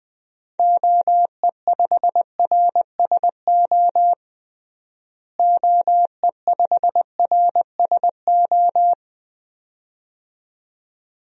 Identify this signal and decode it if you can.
Morse code, heard as a keyed tone, about 20 wpm, 700 Hz. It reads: OE5RSO OE5RSO
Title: DXCC-100 Call Signs - 6 Characters - Repeated 2x - 20wpm